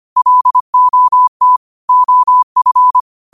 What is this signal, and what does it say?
Morse code, heard as a keyed tone, about 25 wpm, 1000 Hz. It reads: LOT OF
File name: qotdfiles / 25wpm